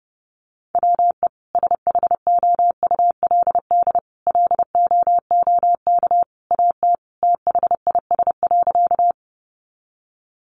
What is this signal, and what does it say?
Morse code, heard as a keyed tone, about 30 wpm, 700 Hz. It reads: WE SHOULD LOOK AT THIS.
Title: Sentences from Top 300 Words 30wpm